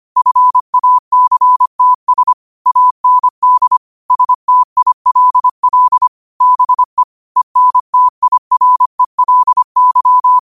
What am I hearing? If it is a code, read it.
Morse code, heard as a keyed tone, about 25 wpm, 1000 Hz. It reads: FACTS AND STILL BE ENTIRELY